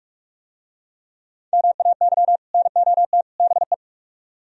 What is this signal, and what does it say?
Morse code, heard as a keyed tone, about 44 wpm, 700 Hz. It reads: MAY NOT BE